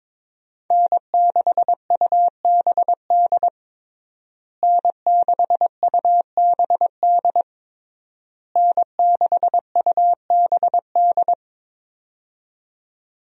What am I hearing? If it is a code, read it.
Morse code, heard as a keyed tone, about 22 wpm, 690 Hz. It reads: N6UBD N6UBD N6UBD
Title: USA Call Signs - 5 Characters - Repeated 3x - 22wpm